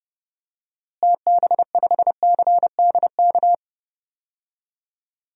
Morse code, heard as a keyed tone, about 30 wpm, 700 Hz. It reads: TB5CDK
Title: DXCC-100 Call Signs - 6 Characters - Repeated 2x - 30wpm